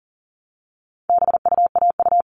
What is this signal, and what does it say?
Morse code, heard as a keyed tone, about 38 wpm, 700 Hz. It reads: BUAU
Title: All Cut Numbers - 4-Digit 40wpm